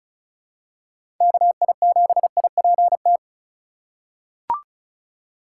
Morse code, heard as a keyed tone, about 36 wpm, 700 Hz. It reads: KI7IPT
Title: USA Call Signs - 3x Character Spacing - 35wpm